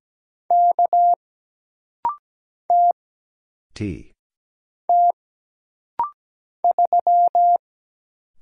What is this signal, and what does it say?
Morse code, heard as a keyed tone, about 17 wpm, 700 Hz. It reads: K T T 3
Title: Single Letter-Number Alt - Rapid-Fire 17wpm